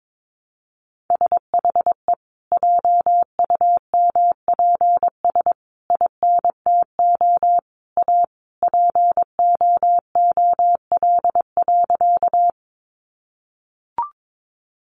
Morse code, heard as a keyed tone, about 22 wpm, 700 Hz. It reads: SHE JUMPS INTO A POOL.
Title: Easy Phrases 22wpm